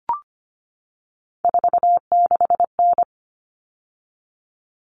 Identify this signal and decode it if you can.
Morse code, heard as a keyed tone, about 25 wpm, 700 Hz. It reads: 46N
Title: All Cut Numbers - 3-Digit 25wpm